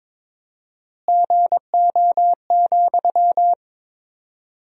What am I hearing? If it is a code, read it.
Morse code, heard as a keyed tone, about 22 wpm, 700 Hz. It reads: GO,
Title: 4x Word-Spacing - Sentences from Top 400 Words - 22wpm